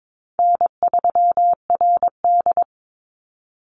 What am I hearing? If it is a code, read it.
Morse code, heard as a keyed tone, about 22 wpm, 700 Hz. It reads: N3RD
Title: Easy Call Signs - 22wpm